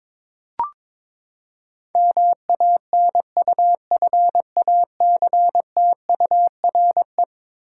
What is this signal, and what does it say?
Morse code, heard as a keyed tone, about 21 wpm, 700 Hz. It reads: MANUFACTURE
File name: practice / 22wpm / Eleven Letter Words - 22wpm